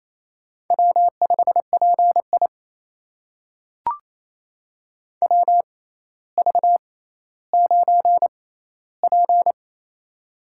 Morse code, heard as a keyed tone, about 27 wpm, spaced as for 14 wpm, 700 Hz. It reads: W5PI WV9P
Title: USA Call Signs - 4 Characters - 6x Character Spacing - 28wpm